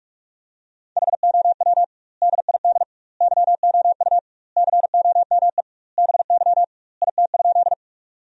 Morse code, heard as a keyed tone, about 46 wpm, 700 Hz. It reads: HOW DID YOU COME BY IT?
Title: 2x Word-Spacing - Sentences from Top 100 Words - 45wpm